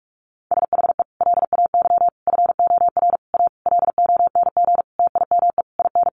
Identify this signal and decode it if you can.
Morse code, heard as a keyed tone, about 44 wpm, 700 Hz. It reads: SHE LAY FOR A LONG TIME IN
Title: Sentences from Top 900 Words 45wpm